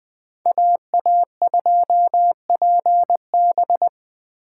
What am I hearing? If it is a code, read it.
Morse code, heard as a keyed tone, about 20 wpm, 700 Hz. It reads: AA2PB
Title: Speed Racing - USA 2x2 Call Signs 17wpm